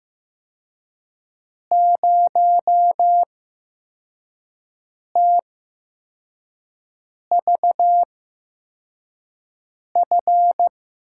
Morse code, heard as a keyed tone, about 15 wpm, spaced as for 4 wpm, 700 Hz. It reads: 0TVF
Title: USA Call Signs - 6 Characters - 8x Character Spacing - 15wpm